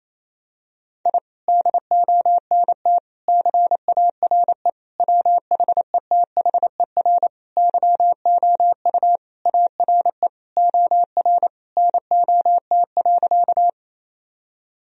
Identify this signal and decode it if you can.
Morse code, heard as a keyed tone, about 28 wpm, 700 Hz. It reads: I DONT CARE WHETHER YOU ARE OR NOT.
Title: Sentences from Top 1000 Words - Review Part 3 - 28wpm